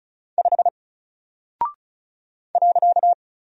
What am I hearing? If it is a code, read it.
Morse code, heard as a keyed tone, about 34 wpm, 700 Hz. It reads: 5 .
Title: Single Character (Letter:Number:Punctuation:Prosign) Alt 35wpm